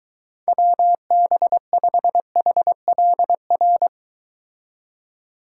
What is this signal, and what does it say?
Morse code, heard as a keyed tone, about 22 wpm, 700 Hz. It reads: WB5HLR